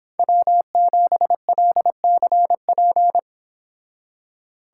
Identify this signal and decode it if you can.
Morse code, heard as a keyed tone, about 26 wpm, 700 Hz. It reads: W7LCP
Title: Speed Racing - USA 1x3 Call Signs 22wpm